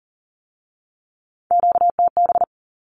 Morse code, heard as a keyed tone, about 38 wpm, 700 Hz. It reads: QTB
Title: Q-Codes with Spoken Meaning - 40wpm